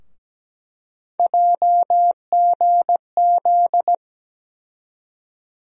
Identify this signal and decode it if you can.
Morse code, heard as a keyed tone, about 17 wpm, 700 Hz. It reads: JGZ
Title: USA Call Sign Postfixes - 3 Letters 17wpm